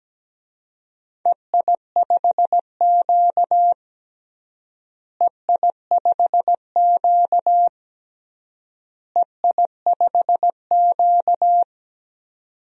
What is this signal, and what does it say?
Morse code, heard as a keyed tone, about 17 wpm, 700 Hz. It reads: EI5Q EI5Q EI5Q
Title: DXCC-100 Call Signs - 4 Characters - Repeated 3x - 17wpm